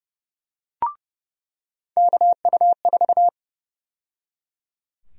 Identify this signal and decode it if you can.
Morse code, heard as a keyed tone, about 30 wpm, 700 Hz. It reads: KU4
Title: USA Call Sign Prefixes - 30wpm